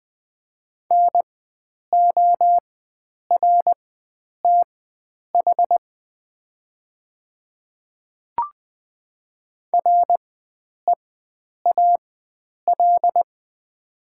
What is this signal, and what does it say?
Morse code, heard as a keyed tone, about 20 wpm, spaced as for 9 wpm, 700 Hz. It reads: NORTH REAL
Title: 4x Character Spacing - Top 300 Words - 20wpm